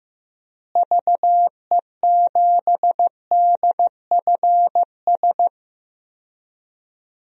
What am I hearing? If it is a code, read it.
Morse code, heard as a keyed tone, about 15 wpm, 700 Hz. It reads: VE7DFS
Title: DXCC-100 Call Signs - 6 Characters - Repeated 3x - 15wpm